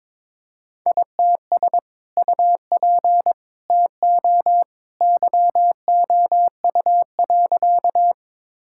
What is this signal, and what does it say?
Morse code, heard as a keyed tone, about 22 wpm, 700 Hz. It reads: ITS UP TO YOU.
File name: practice / 22wpm / Sentences with 2-Word Phrases - 22wpm